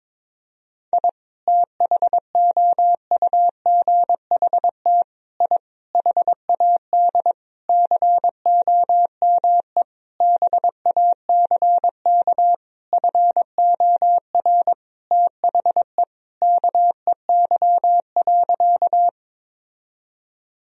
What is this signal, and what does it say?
Morse code, heard as a keyed tone, about 22 wpm, 700 Hz. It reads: I THOUGHT I HAD COME BACK FOR THE KEY.
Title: Sentences from Top 1000 Words - Review Part 4 - 22wpm